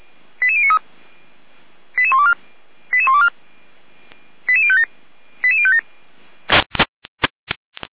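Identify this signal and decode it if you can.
Five-tone selective call (ZVEI), heard as quick runs of five tones, tones of about 70 ms. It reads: 88093 88124 88124 88057 88057
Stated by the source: ZVEI five-tone selective call received on SDR.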